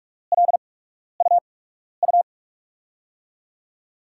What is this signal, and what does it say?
Morse code, heard as a keyed tone, about 46 wpm, spaced as for 12 wpm, 700 Hz. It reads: LUU
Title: USA Call Signs - 5 Characters - 8x Character Spacing - 45wpm